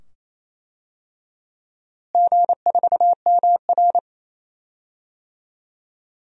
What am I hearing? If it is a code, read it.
Morse code, heard as a keyed tone, about 28 wpm, 700 Hz. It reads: G4MR
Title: DXCC-100 Call Signs - 4 Characters - Repeated 2x - 28wpm